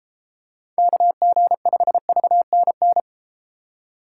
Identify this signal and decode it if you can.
Morse code, heard as a keyed tone, about 31 wpm, 700 Hz. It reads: KG5VNN